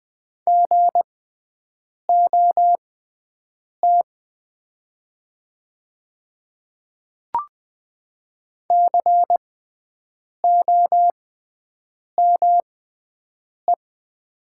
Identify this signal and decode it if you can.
Morse code, heard as a keyed tone, about 20 wpm, spaced as for 7 wpm, 700 Hz. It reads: GOT COME
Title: 6x Character Spacing - Top 300 Words - 20wpm